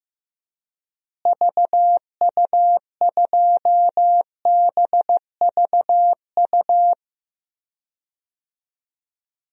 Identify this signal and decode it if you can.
Morse code, heard as a keyed tone, about 15 wpm, 700 Hz. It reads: VU2BVU
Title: DXCC-100 Call Signs - 6 Characters - Repeated 3x - 15wpm